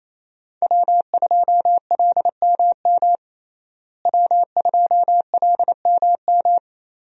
Morse code, heard as a keyed tone, about 28 wpm, 700 Hz. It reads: W2LMM W2LMM
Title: USA Call Signs - 5 Characters - Repeated 2x - 28wpm